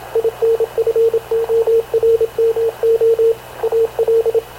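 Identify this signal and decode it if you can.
Morse code, heard as a keyed tone, about 27 wpm, 460 Hz, with radio noise. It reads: INFORMO AL
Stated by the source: Havana Radio / CLA sending Spanish press (PX) in Morse, recorded by Radio Officer Paul Dumesnil onboard CNAV QUEST/CZDO.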